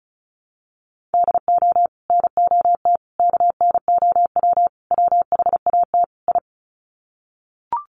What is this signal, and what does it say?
Morse code, heard as a keyed tone, about 34 wpm, 700 Hz. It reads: DO NOT KNOW WHAT I
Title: Sets of 5 Words 35wpm